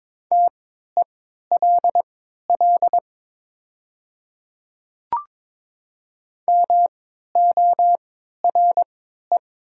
Morse code, heard as a keyed tone, about 22 wpm, spaced as for 12 wpm, 700 Hz. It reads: TELL MORE
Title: 3x Character Spacing - Top 300 Words - 22wpm